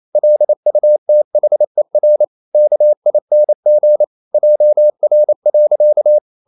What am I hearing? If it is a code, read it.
Morse code, heard as a keyed tone, about 28 wpm, 600 Hz. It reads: LUTHER KING JR.
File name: CW / Practice / EnglishQuotations0128